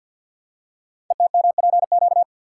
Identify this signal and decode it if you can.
Morse code, heard as a keyed tone, about 48 wpm, 700 Hz. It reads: ETMPQ
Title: Code Groups - 5 Letters - 50wpm